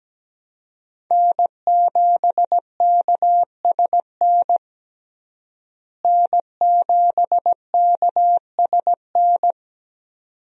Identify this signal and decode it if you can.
Morse code, heard as a keyed tone, about 17 wpm, 700 Hz. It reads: N7KSN N7KSN
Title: USA Call Signs - 5 Characters - Repeated 2x - 17wpm